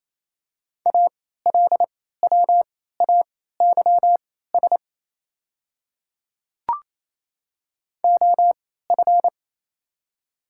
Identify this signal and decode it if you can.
Morse code, heard as a keyed tone, about 28 wpm, spaced as for 16 wpm, 700 Hz. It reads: ALWAYS OF
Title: 3x Character Spacing - Top 300 Words - 28wpm